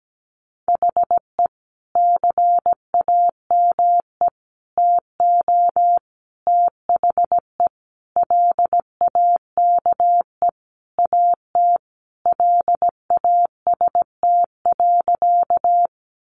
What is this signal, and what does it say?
Morse code, heard as a keyed tone, about 17 wpm, 700 Hz. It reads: HE CAME TO THE LAKE AT LAST.
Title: Sentences from Top 1100 Words 17wpm